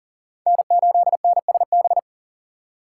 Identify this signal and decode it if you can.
Morse code, heard as a keyed tone, about 37 wpm, 700 Hz. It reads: N8NSB
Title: USA Call Signs - 5 Characters - 2x Character Spacing - 40wpm